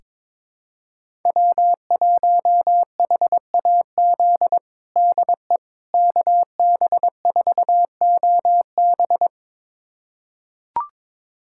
Morse code, heard as a keyed tone, about 22 wpm, 700 Hz. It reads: W1HAZ DE KB4OB
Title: USA Callsign DE Callsign - 5 Characters - 22wpm